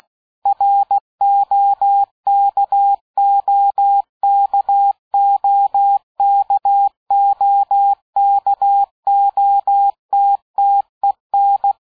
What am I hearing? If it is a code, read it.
Morse code, heard as a keyed tone, about 16 wpm, 800 Hz. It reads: ROKOKOKOKOTTEN
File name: dk5ke / Fingerbrecher 75BpM